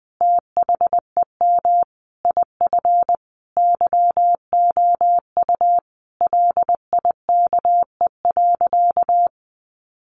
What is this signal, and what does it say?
Morse code, heard as a keyed tone, about 20 wpm, 700 Hz. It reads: THEM IF YOU LIKE.
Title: Sentences from Top 1300 Words 20wpm